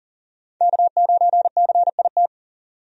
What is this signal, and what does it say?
Morse code, heard as a keyed tone, about 40 wpm, 700 Hz. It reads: K9CIT
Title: USA Call Signs - 5 Characters - 40wpm